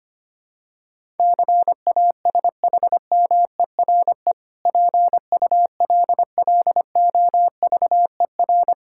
Morse code, heard as a keyed tone, about 25 wpm, 700 Hz. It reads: CASHMERE PULLOVER
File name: practice / 25wpm / Difficult Words with Hints - 25wpm